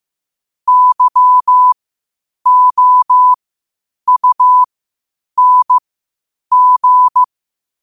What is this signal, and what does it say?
Morse code, heard as a keyed tone, about 15 wpm, spaced as for 8 wpm, 1000 Hz. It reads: YOUNG
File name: qotdfiles / 8wpm